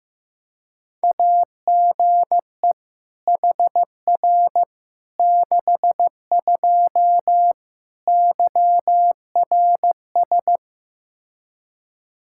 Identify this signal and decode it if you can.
Morse code, heard as a keyed tone, about 15 wpm, 700 Hz. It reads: AGE HR 62 YRS
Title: Practice QSO - Exchange 3 - More Info - 15wpm